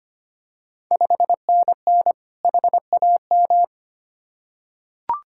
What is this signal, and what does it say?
Morse code, heard as a keyed tone, about 25 wpm, 690 Hz. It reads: 5NN HAM
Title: Ontario QSO Party Exchange 25wpm